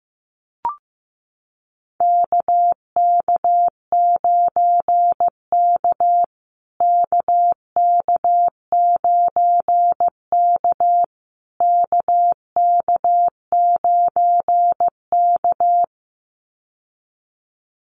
Morse code, heard as a keyed tone, about 15 wpm, 700 Hz. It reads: KK9K KK9K KK9K
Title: USA Call Signs Repeated 3x 15wpm